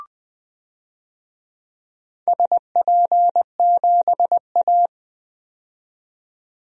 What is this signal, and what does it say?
Morse code, heard as a keyed tone, about 20 wpm, 700 Hz. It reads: SP7A
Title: DXCC-100 Call Signs - 4 Characters - 20wpm